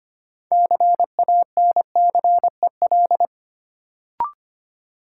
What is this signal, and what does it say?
Morse code, heard as a keyed tone, about 25 wpm, 700 Hz. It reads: CANCEL
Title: Sentences from Top 1200 Words - New Word Drill Alt - 25wpm